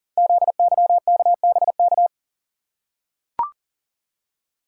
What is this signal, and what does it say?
Morse code, heard as a keyed tone, about 42 wpm, 700 Hz. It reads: ZYKBK
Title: Code Groups - 5 Letters - 40wpm